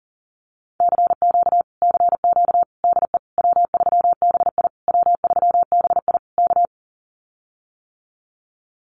Morse code, heard as a keyed tone, about 40 wpm, 700 Hz. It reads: CQ CQ DE W3BI W3BI K
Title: Common QSO Phrases Encore 40wpm